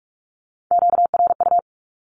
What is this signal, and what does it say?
Morse code, heard as a keyed tone, about 44 wpm, 700 Hz. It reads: QRU
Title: Q-Codes Repeated 3x 45wpm